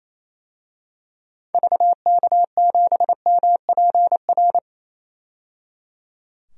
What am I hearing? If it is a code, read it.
Morse code, heard as a keyed tone, about 28 wpm, 700 Hz. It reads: VK7MPR
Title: DXCC-100 Call Signs - 6 Characters - 28wpm